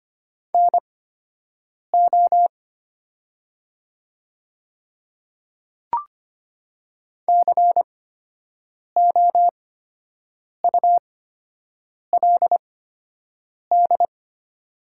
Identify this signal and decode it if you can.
Morse code, heard as a keyed tone, about 25 wpm, spaced as for 7 wpm, 700 Hz. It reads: NO COULD
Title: 8x Character Spacing - Top 100 Words - 25wpm